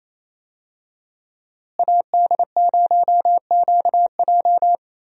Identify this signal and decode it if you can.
Morse code, heard as a keyed tone, about 28 wpm, 700 Hz. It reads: AD0QJ
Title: USA Call Signs - 5 Characters - Repeated 2x - 28wpm